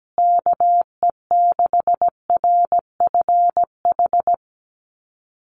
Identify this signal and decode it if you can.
Morse code, heard as a keyed tone, about 17 wpm, 700 Hz. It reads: KE6RFH
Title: USA Call Signs Repeated 3x 17wpm